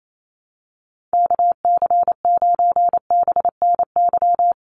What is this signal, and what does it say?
Morse code, heard as a keyed tone, about 28 wpm, 700 Hz. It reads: KC9BNY
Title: USA Call Signs Repeated 2x 28wpm